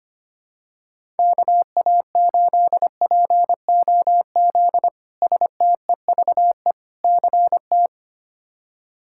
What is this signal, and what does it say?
Morse code, heard as a keyed tone, about 25 wpm, 700 Hz. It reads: KA8POZ STEVE CT
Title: K1USN SST QSO Practice - 25wpm